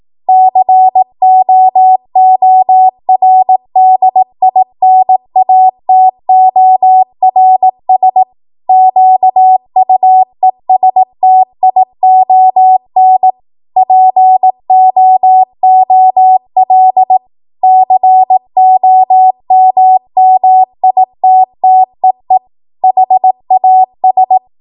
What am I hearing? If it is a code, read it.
Morse code, heard as a keyed tone, about 18 wpm, 750 Hz. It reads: COORDINATORS QUESTION POOL COMMITTEE HAS